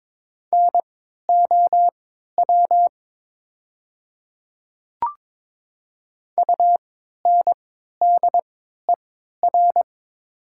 Morse code, heard as a keyed tone, about 22 wpm, spaced as for 13 wpm, 700 Hz. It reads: NOW UNDER
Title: 3x Character Spacing - Top 200 Words - 22wpm